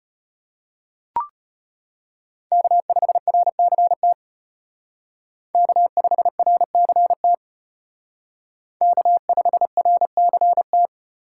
Morse code, heard as a keyed tone, about 33 wpm, 700 Hz. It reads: K5RCT K5RCT K5RCT